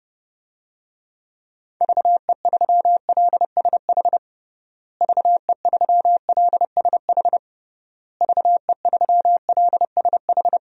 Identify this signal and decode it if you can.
Morse code, heard as a keyed tone, about 30 wpm, 700 Hz. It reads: VE3LSH VE3LSH VE3LSH
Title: DXCC-100 Call Signs - 6 Characters - Repeated 3x - 30wpm